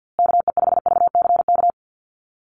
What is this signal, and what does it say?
Morse code, heard as a keyed tone, about 48 wpm, 700 Hz. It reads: KE5VCK
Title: USA Call Signs Repeated 2x 50wpm